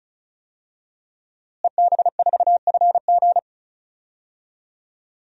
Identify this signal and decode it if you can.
Morse code, heard as a keyed tone, about 36 wpm, 700 Hz. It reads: EB4FG
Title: DXCC-100 Call Signs - 5 Characters - Repeated 3x - 35wpm